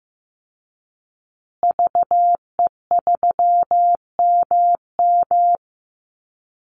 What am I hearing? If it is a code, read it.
Morse code, heard as a keyed tone, about 14 wpm, 700 Hz. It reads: VE3MM
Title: Call Signs Repeated 2x - 15wpm